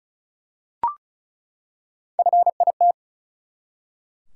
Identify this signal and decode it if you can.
Morse code, heard as a keyed tone, about 34 wpm, 700 Hz. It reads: FIT
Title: Three Letter Words - 35wpm